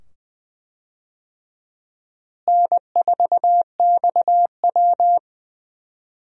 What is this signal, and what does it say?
Morse code, heard as a keyed tone, about 20 wpm, 700 Hz. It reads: N4XW
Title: USA Call Signs - 4 Characters - Repeated 2x - 20wpm